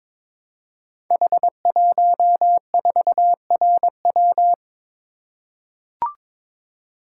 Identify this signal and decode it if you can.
Morse code, heard as a keyed tone, about 22 wpm, 700 Hz. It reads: H14RW
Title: Code Groups - 5 Letter:Numbers - 22wpm